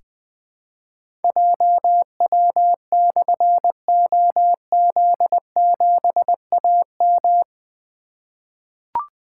Morse code, heard as a keyed tone, about 20 wpm, 700 Hz. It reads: JW/OZ7AM
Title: Difficult Call Signs 20wpm